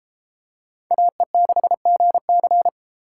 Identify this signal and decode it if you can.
Morse code, heard as a keyed tone, about 33 wpm, 700 Hz. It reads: AE6GC